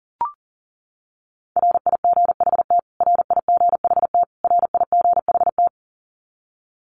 Morse code, heard as a keyed tone, about 40 wpm, 700 Hz. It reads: RIGHT RIGHT RIGHT
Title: Top 200 Words Repeated 40wpm